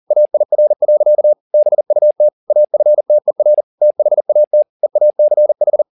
Morse code, heard as a keyed tone, about 40 wpm, 600 Hz. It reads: AIR. BUT AFTER THAT EACH
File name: the old man and the sea 2 40wpm